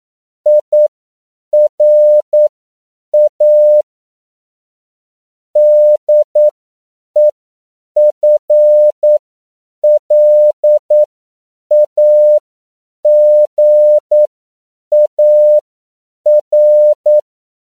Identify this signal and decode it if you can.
Morse code, heard as a keyed tone, about 9 wpm, spaced as for 7 wpm, 600 Hz. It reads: IRA DEFLAGAR
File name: CW / 10PPM